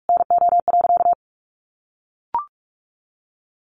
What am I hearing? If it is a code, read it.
Morse code, heard as a keyed tone, about 44 wpm, 700 Hz. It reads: NO.
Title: A Sound Of Thunder 45wpm